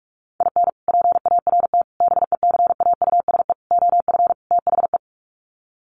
Morse code, heard as a keyed tone, about 44 wpm, 700 Hz. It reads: IN PART BECAUSE OF THE
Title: Sets of 5 Words Encore2 45wpm